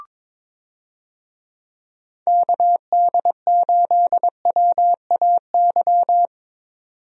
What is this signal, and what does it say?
Morse code, heard as a keyed tone, about 22 wpm, 700 Hz. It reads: KD8WAY
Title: USA Call Signs - 6 Characters - Repeated 2x - 22wpm